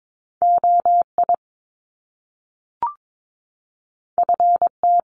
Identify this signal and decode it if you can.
Morse code, heard as a keyed tone, about 22 wpm, 700 Hz. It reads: OI FT
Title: Top Two Letter Combinations in Words 22wpm